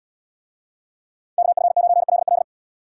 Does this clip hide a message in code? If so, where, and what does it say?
Yes: Morse code from 1.4 to 2.4 s, heard as a keyed tone, about 76 wpm, 700 Hz. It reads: KF9LL